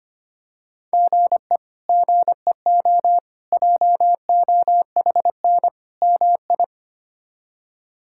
Morse code, heard as a keyed tone, about 25 wpm, 700 Hz. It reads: GE GEO JOHN MI
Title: K1USN SST QSO Practice - 25wpm